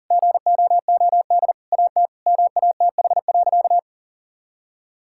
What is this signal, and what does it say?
Morse code, heard as a keyed tone, about 40 wpm, 700 Hz. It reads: GOOD AT MATH.
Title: Sentences from Top 1000 Words - Review Part 6 - 40wpm